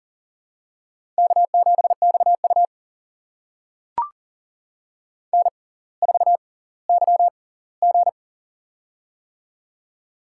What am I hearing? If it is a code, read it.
Morse code, heard as a keyed tone, about 40 wpm, spaced as for 20 wpm, 700 Hz. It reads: K7XU N4YG
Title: USA Call Signs - 4 Characters - 6x Character Spacing - 40wpm